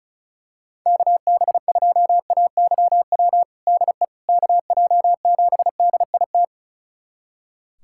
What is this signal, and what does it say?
Morse code, heard as a keyed tone, about 34 wpm, 700 Hz. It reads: KB2AYW DE KJ7DIT
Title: USA Callsign DE Callsign - 6 Characters - 35wpm